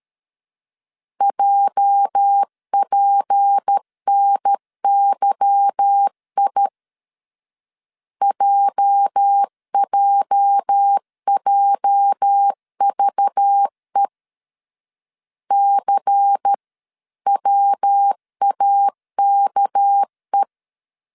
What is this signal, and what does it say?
Morse code, heard as a keyed tone, about 13 wpm, 800 Hz. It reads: JPNYI JJJVE CWAKE
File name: LEZIONI / 8 YJ45 / 8 YJ 12wpm 2235